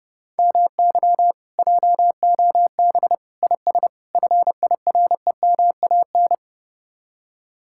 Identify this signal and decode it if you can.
Morse code, heard as a keyed tone, about 36 wpm, 700 Hz. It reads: MY JOB IS FIREMAN
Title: Speed-Racing Common QSO Phrases - 30wpm